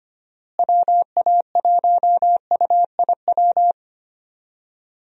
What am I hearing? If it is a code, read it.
Morse code, heard as a keyed tone, about 25 wpm, 700 Hz. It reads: WA1UIW
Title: USA Call Signs - 6 Characters - Repeated 3x - 25wpm